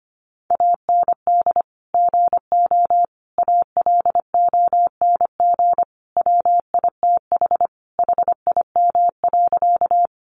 Morse code, heard as a keyed tone, about 25 wpm, 700 Hz. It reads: AND GO ALONG WITH HIM.
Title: Sentences from Top 700 Words 25wpm